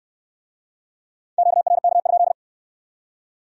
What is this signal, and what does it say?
Morse code, heard as a keyed tone, about 65 wpm, 700 Hz. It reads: QRK?